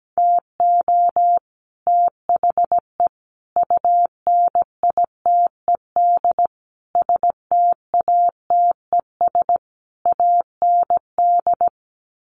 Morse code, heard as a keyed tone, about 17 wpm, 700 Hz. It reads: TO THE UNITED STATES AND
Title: Sets of 5 Words Encore 17wpm